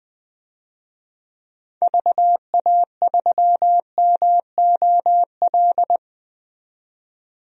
Morse code, heard as a keyed tone, about 21 wpm, 700 Hz. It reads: VA3MOL
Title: DXCC-100 Call Signs - 6 Characters - 20wpm